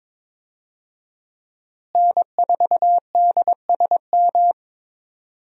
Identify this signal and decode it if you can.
Morse code, heard as a keyed tone, about 22 wpm, 700 Hz. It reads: N4DSM
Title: USA Call Signs - 5 Characters - Repeated 3x - 22wpm